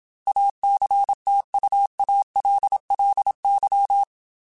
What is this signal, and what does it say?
Morse code, heard as a keyed tone, about 27 wpm, 800 Hz. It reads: ACTUALLY
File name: Technical Subjects / Technical Subjects-mp3s / AND Gate @25